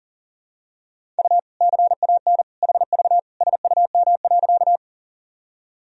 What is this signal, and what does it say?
Morse code, heard as a keyed tone, about 40 wpm, 700 Hz. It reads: U CAN HV SUM.
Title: Sentences from Top 100 Words and Abbreviations 40wpm